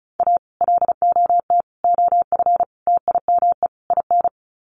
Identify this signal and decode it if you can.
Morse code, heard as a keyed tone, about 34 wpm, 700 Hz. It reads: A LOT OF TIME IN
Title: Sets of 5 Words Encore2 35wpm